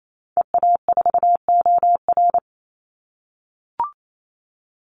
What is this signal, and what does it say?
Morse code, heard as a keyed tone, about 28 wpm, 700 Hz. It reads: EA4OR
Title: Call Signs - 28wpm